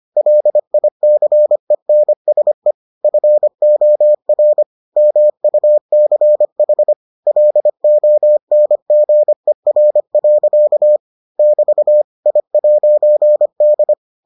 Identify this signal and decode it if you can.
Morse code, heard as a keyed tone, about 25 wpm, 600 Hz. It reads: LICENSE FOR MUCH LONGER. <BT> I'D